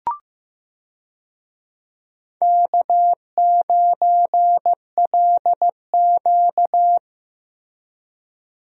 Morse code, heard as a keyed tone, about 15 wpm, 700 Hz. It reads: K9LQ
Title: USA Call Signs - 4 Characters - Repeated 3x - 15wpm